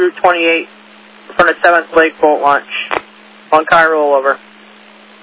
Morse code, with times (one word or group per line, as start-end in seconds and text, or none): none